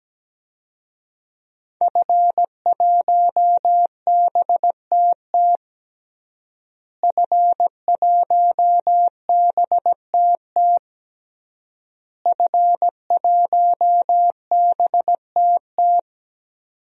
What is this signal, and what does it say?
Morse code, heard as a keyed tone, about 17 wpm, 700 Hz. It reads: F1BTT F1BTT F1BTT
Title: DXCC-100 Call Signs - 5 Characters - Repeated 3x - 17wpm